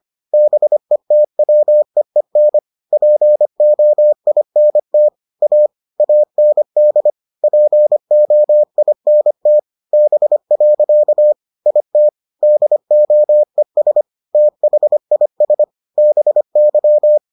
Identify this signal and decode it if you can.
Morse code, heard as a keyed tone, about 25 wpm, 600 Hz. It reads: BETWEEN POINT A AND POINT B. IT DOES THIS BY